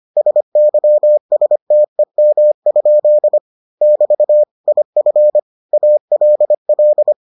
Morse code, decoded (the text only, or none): SYSTEM? <BT> IF ALL